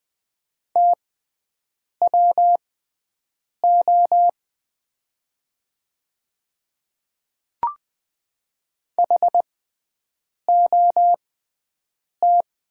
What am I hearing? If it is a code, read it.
Morse code, heard as a keyed tone, about 20 wpm, spaced as for 7 wpm, 700 Hz. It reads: TWO HOT
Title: 6x Character Spacing - Top 100 Words - 20wpm